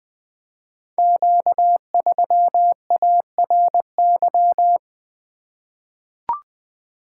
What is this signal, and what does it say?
Morse code, heard as a keyed tone, about 20 wpm, 700 Hz. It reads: Q3ARY
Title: Code Groups - 5 Letter:Numbers - 20wpm